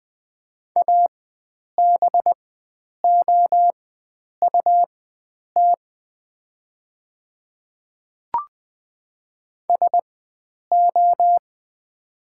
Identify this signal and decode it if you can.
Morse code, heard as a keyed tone, about 20 wpm, spaced as for 9 wpm, 700 Hz. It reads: ABOUT SO
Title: 4x Character Spacing - Top 100 Words - 20wpm